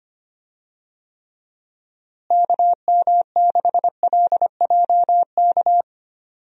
Morse code, heard as a keyed tone, about 25 wpm, 700 Hz. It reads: KM6LJK
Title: USA Call Signs - 6 Characters - Repeated 2x - 25wpm